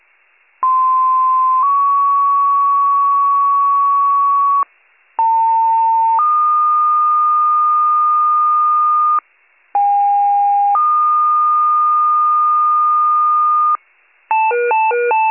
Two-tone page, heard as a one-second tone, then a three-second tone, about 1040 then 1120 Hz, about 900 then 1220 Hz, about 810 then 1220 Hz. It is heard from 0.6 to 13.8 s.